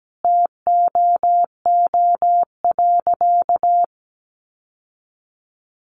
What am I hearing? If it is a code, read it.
Morse code, heard as a keyed tone, about 17 wpm, 700 Hz. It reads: TOO.
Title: Sentences from Top 1400 Words 17wpm